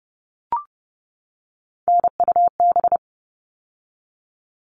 Morse code, heard as a keyed tone, about 30 wpm, 700 Hz. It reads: NUB
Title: All Cut Numbers - 3-Digit 30wpm